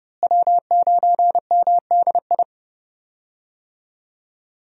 Morse code, heard as a keyed tone, about 31 wpm, 700 Hz. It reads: W9MDI
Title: USA Call Signs - 5 Characters - Repeated 3x - 30wpm